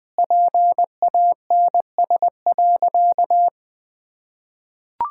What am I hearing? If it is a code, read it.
Morse code, heard as a keyed tone, about 19 wpm, 700 Hz. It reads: PANS.
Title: Pangram Challenge v2 - 20wpm